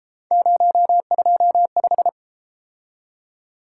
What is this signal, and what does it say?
Morse code, heard as a keyed tone, about 33 wpm, 700 Hz. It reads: 025